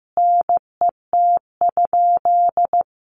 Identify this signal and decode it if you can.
Morse code, heard as a keyed tone, about 15 wpm, 700 Hz. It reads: NET?
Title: Sentences from Top 1400 Words 15wpm